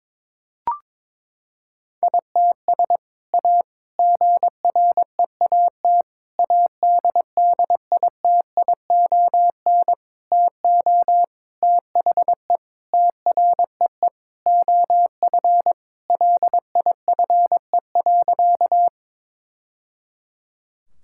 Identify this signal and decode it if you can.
Morse code, heard as a keyed tone, about 22 wpm, 700 Hz. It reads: ITS A GREAT ADDITION TO THE TREE OF LIFE.
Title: Sentences from Top 1000 Words - Review Part 6 - 22wpm